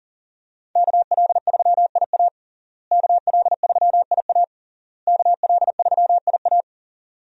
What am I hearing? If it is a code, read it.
Morse code, heard as a keyed tone, about 37 wpm, 700 Hz. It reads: KL3IA KL3IA KL3IA
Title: USA Call Signs - 5 Characters - Repeated 3x - 40wpm